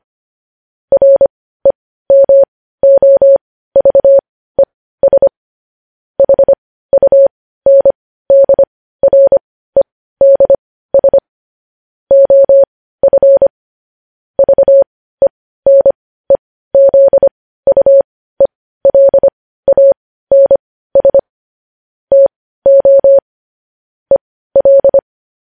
Morse code, heard as a keyed tone, about 25 wpm, spaced as for 15 wpm, 550 Hz. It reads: REMOVES HUNDREDS OF VENEZUELANS TO EL
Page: news-headlines-in-morse-code-at-15-wpm